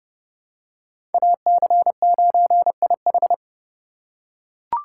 Morse code, heard as a keyed tone, about 30 wpm, 700 Hz. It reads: AC9IH
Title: USA Call Signs - 5 Characters - 4x Character Spacing - 30wpm